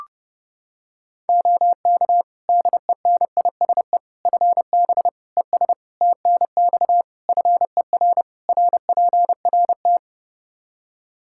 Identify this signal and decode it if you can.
Morse code, heard as a keyed tone, about 30 wpm, 700 Hz. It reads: OK DENISE FB ES TNX FER RPRT
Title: K1USN SST QSO Practice 30wpm